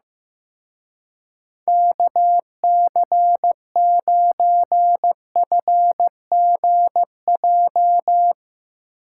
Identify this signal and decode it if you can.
Morse code, heard as a keyed tone, about 15 wpm, 700 Hz. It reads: KC9FGJ
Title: Speed Racing - USA Call Signs 15wpm